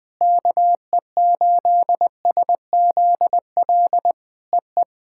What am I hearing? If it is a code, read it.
Morse code, heard as a keyed tone, about 19 wpm, 700 Hz. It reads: KE8SZL EE
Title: POTA QSO Practice - 20wpm